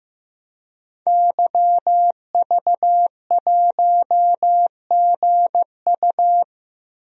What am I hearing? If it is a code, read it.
Morse code, heard as a keyed tone, about 15 wpm, 700 Hz. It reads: YV1GU
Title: DXCC-100 Call Signs - 5 Characters - 15wpm